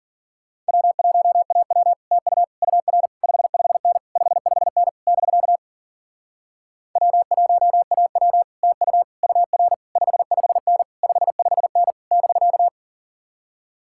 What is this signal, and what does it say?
Morse code, heard as a keyed tone, about 52 wpm, 700 Hz. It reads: W1AW TU UR 55N 55N <BK> W1AW TU UR 55N 55N <BK>